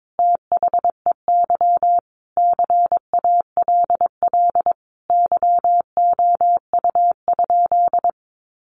Morse code, heard as a keyed tone, about 22 wpm, 700 Hz. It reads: THEY CALL YOU?
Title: Sentences from Top 100 Words 22wpm